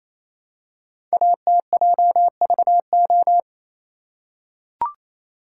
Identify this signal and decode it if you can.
Morse code, heard as a keyed tone, about 27 wpm, 700 Hz. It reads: ATJVO
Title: Code Groups - 5 Letters - 28wpm